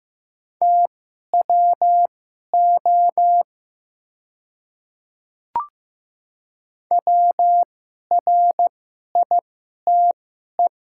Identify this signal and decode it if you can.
Morse code, heard as a keyed tone, about 15 wpm, spaced as for 11 wpm, 700 Hz. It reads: TWO WRITE
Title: 2x Character Spacing - Top 100 Words - 15wpm